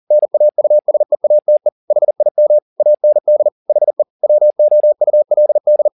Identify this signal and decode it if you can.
Morse code, heard as a keyed tone, about 40 wpm, 600 Hz. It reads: NAUSEATE HIM AND HE WOULD
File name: the old man and the sea 2 40wpm